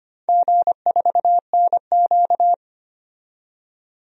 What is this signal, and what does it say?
Morse code, heard as a keyed tone, about 25 wpm, 700 Hz. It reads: G4NQ
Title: DXCC-100 Call Signs - 4 Characters - 25wpm